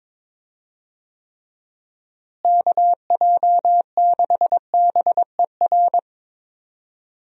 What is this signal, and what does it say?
Morse code, heard as a keyed tone, about 22 wpm, 700 Hz. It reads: KJ6BER
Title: USA Call Signs - 6 Characters - Repeated 3x - 22wpm